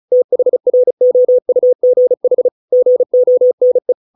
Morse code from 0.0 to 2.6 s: THROUGH